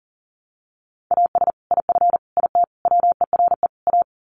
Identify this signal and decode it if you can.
Morse code, heard as a keyed tone, about 40 wpm, 700 Hz. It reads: AS IF IT WERE A
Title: Sets of 5 Words 40wpm